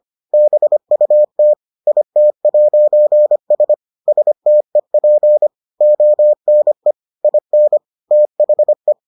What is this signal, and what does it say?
Morse code, heard as a keyed tone, about 25 wpm, 600 Hz. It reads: BUT IT'S STEP ONE IN THE